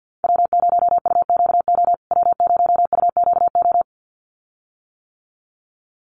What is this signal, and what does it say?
Morse code, heard as a keyed tone, about 48 wpm, 700 Hz. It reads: W0UQO W0UQO
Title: USA Call Signs Repeated 2x 50wpm